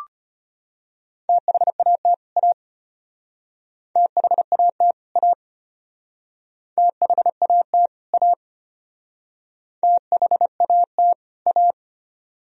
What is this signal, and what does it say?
Morse code, heard as a keyed tone, about 33 wpm, 700 Hz. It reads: THAT A THAT A THAT A THAT A